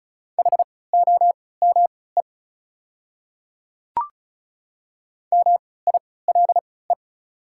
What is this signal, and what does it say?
Morse code, heard as a keyed tone, about 36 wpm, spaced as for 20 wpm, 690 Hz. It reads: HOME MILE
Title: 3x Character Spacing - Top 300 Words - 35wpm